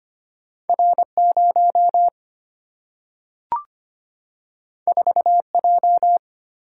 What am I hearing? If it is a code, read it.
Morse code, heard as a keyed tone, about 25 wpm, 700 Hz. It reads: R0 4J
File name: Two Letter-Number Combinations - 25wpm